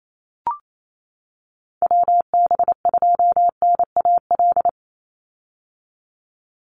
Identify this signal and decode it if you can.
Morse code, heard as a keyed tone, about 28 wpm, 700 Hz. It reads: WB2NAL
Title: USA Call Signs 28wpm